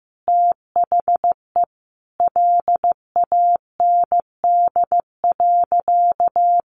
Morse code, heard as a keyed tone, about 15 wpm, 700 Hz. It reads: THE LAND.
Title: Sentences from Top 1500 Words 15wpm